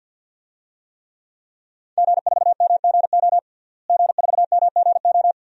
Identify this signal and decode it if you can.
Morse code, heard as a keyed tone, about 52 wpm, 700 Hz. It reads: G4MGO G4MGO
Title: DXCC-100 Call Signs - 5 Characters - Repeated 2x - 50wpm